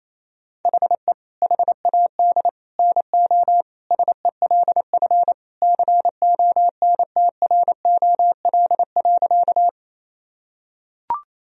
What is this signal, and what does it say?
Morse code, heard as a keyed tone, about 27 wpm, 700 Hz. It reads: HE HAD NO SELF CONTROL.
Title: Sentences from Top 1000 Words - Review Part 3 - 28wpm